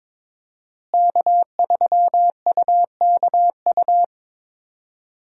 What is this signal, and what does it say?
Morse code, heard as a keyed tone, about 22 wpm, 700 Hz. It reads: K3UKU
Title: USA Call Signs - 5 Characters - Repeated 3x - 22wpm